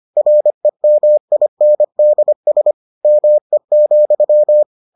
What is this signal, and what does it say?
Morse code, heard as a keyed tone, about 25 wpm, 600 Hz. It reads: REMINDS ME,